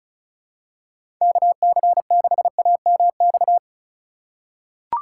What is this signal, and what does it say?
Morse code, heard as a keyed tone, about 36 wpm, 700 Hz. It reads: KC6AMX
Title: USA Call Signs - 6 Characters - 2x Character Spacing - 35wpm